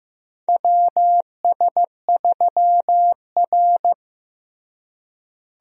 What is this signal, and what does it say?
Morse code, heard as a keyed tone, about 15 wpm, 700 Hz. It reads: WS3R
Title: USA Call Signs - 4 Characters - Repeated 3x - 15wpm